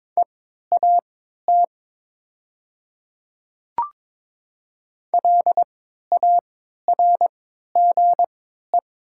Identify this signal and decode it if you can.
Morse code, heard as a keyed tone, about 22 wpm, spaced as for 12 wpm, 700 Hz. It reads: EAT LARGE
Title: 3x Character Spacing - Top 300 Words - 22wpm